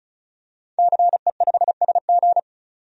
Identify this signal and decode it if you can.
Morse code, heard as a keyed tone, about 36 wpm, 700 Hz. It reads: CE5SG
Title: DXCC-100 Call Signs - 5 Characters - 35wpm